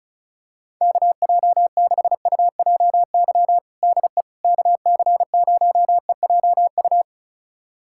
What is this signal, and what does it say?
Morse code, heard as a keyed tone, about 34 wpm, 690 Hz. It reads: KJ6UJY DE KC0EJU
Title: USA Callsign DE Callsign - 6 Characters - 35wpm